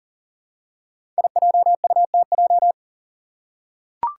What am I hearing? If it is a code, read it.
Morse code, heard as a keyed tone, about 40 wpm, 700 Hz. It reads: IJUTJ
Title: Five Challenging Letter-Number Combinations 40wpm